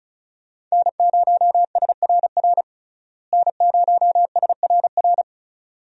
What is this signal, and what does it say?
Morse code, heard as a keyed tone, about 36 wpm, 700 Hz. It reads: N0SRR N0SRR
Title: USA Call Signs - 5 Characters - Repeated 2x - 35wpm